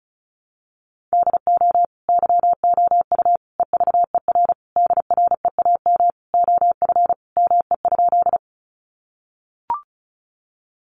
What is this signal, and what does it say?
Morse code, heard as a keyed tone, about 34 wpm, 700 Hz. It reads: DO YOU EVER DREAM OF ME?
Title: Sentences from Top 1500 Words 35wpm